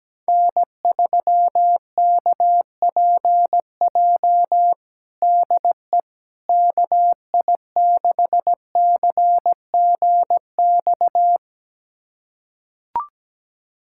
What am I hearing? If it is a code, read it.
Morse code, heard as a keyed tone, about 17 wpm, 700 Hz. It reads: N3KPJ DE KI6CGX
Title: USA Callsign DE Callsign 17wpm